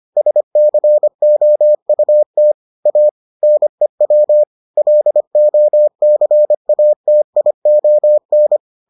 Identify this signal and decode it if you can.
Morse code, heard as a keyed tone, about 25 wpm, 600 Hz. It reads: SCOUT A NEW LOCATION